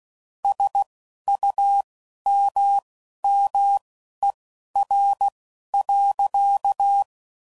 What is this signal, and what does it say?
Morse code, heard as a keyed tone, about 16 wpm, spaced as for 11 wpm, 800 Hz. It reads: SUMMER.
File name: CW Code Groups @10WPM / QSOs / QSO WD0FBT